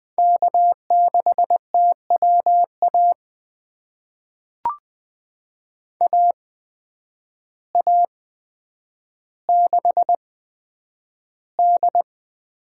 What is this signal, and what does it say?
Morse code, heard as a keyed tone, about 20 wpm, spaced as for 9 wpm, 700 Hz. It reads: K6TWA AA6D
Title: USA Call Signs - 8x Character Spacing - 20wpm